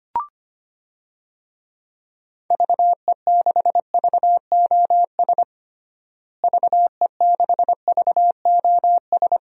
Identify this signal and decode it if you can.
Morse code, heard as a keyed tone, about 25 wpm, 700 Hz. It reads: VE6VOS VE6VOS
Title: DXCC-100 Call Signs - 6 Characters - Repeated 2x - 25wpm